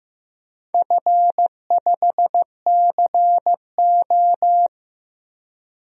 Morse code, heard as a keyed tone, about 15 wpm, 700 Hz. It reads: F5CO
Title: DXCC-100 Call Signs - 4 Characters - Repeated 2x - 15wpm